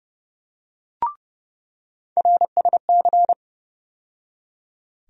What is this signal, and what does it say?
Morse code, heard as a keyed tone, about 30 wpm, 700 Hz. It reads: RSC
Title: USA Call Sign Postfixes - 3 Letters 30wpm